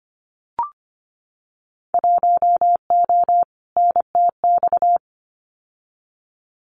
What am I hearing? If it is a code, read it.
Morse code, heard as a keyed tone, about 25 wpm, 700 Hz. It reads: 1O NTX
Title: Winter Field Day Exchanges 25wpm